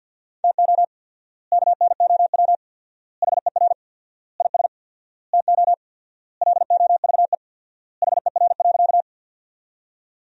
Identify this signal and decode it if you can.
Morse code, heard as a keyed tone, about 52 wpm, 700 Hz. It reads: TO KNOW HER IS TO LOVE HER.
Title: 4x Word-Spacing - Sentences from Top 400 Words - 50wpm